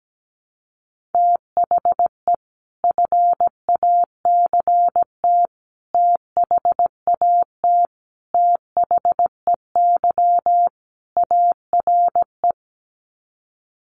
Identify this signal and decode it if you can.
Morse code, heard as a keyed tone, about 17 wpm, 700 Hz. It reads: THE FACT THAT THEY ARE
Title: Sets of 5 Words Encore2 17wpm